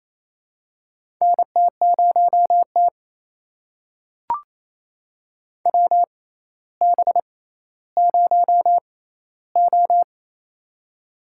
Morse code, heard as a keyed tone, about 28 wpm, spaced as for 14 wpm, 700 Hz. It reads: NT0T WB0O
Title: USA Call Signs - 4 Characters - 6x Character Spacing - 28wpm